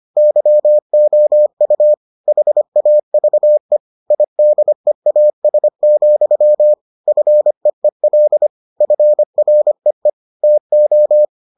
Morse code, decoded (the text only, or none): YOU HAVE IDEAS, FEEL FREE TO